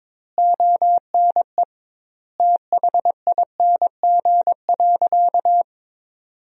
Morse code, heard as a keyed tone, about 22 wpm, 700 Hz. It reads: ONE THING.
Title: 2x Word-Spacing - Sentences from Top 300 Words - 22wpm